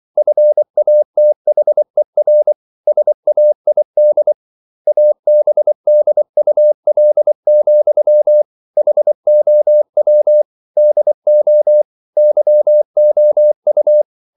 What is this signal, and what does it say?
Morse code, heard as a keyed tone, about 24 wpm, 600 Hz. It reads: FATHER SAID ABDUL, HOW DO YOU